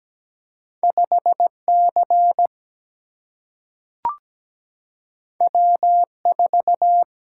Morse code, heard as a keyed tone, about 17 wpm, 700 Hz. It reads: 5C W4
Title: Single-Letter Single-Number 17wpm